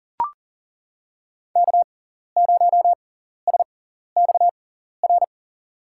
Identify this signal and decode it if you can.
Morse code, heard as a keyed tone, about 40 wpm, spaced as for 14 wpm, 700 Hz. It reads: K0SXR
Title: USA Call Signs - 5 Characters - 6x Character Spacing - 40wpm